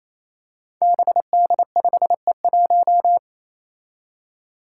Morse code, heard as a keyed tone, about 27 wpm, 700 Hz. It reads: BD5E1
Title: Five Challenging Letter-Number Combinations 28wpm